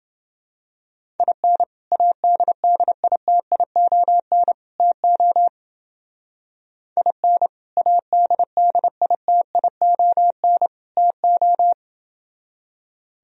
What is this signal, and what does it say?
Morse code, heard as a keyed tone, about 28 wpm, 700 Hz. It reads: IN ADDITION TO IN ADDITION TO